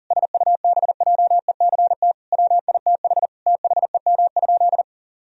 Morse code, heard as a keyed tone, about 37 wpm, 700 Hz. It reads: SUBJECT WITH THEM?
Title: Sentences from Top 1000 Words - Review Part 1 - 40wpm